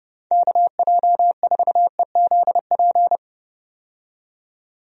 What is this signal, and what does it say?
Morse code, heard as a keyed tone, about 30 wpm, 700 Hz. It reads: KJ4EZP
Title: USA Call Signs - 6 Characters - 30wpm